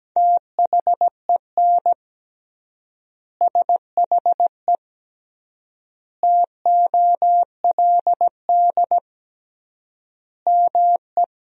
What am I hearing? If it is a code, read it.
Morse code, heard as a keyed tone, about 17 wpm, 700 Hz. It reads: THEN SHE TOLD ME
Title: 3x Word-Spacing - Sentences from Top 400 Words - 17wpm